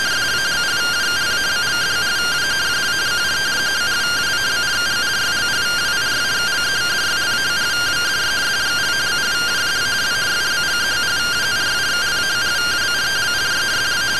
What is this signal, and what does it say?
Two-tone teleprinter signal, about 170 Hz shift.